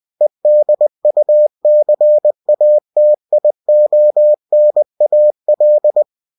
Morse code, heard as a keyed tone, about 20 wpm, 600 Hz. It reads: EDUCATIONAL